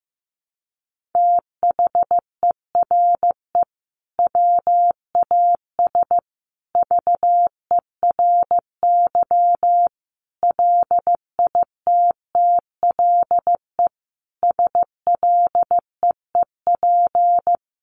Morse code, heard as a keyed tone, about 15 wpm, 700 Hz. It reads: THERE WAS VERY LITTLE SLEEP
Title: Sentences from Top 1400 Words 15wpm